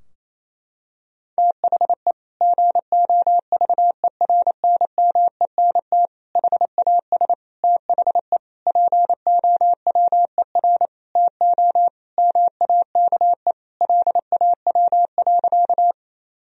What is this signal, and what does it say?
Morse code, heard as a keyed tone, about 28 wpm, 700 Hz. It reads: THE GOVERNMENT HAS THE POWER TO MAKE LAW.
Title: Sentences from Top 1000 Words - Review Part 3 - 28wpm